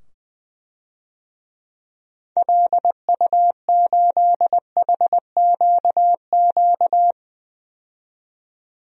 Morse code, heard as a keyed tone, about 20 wpm, 700 Hz. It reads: LU8HQQ
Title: DXCC-100 Call Signs - 6 Characters - 20wpm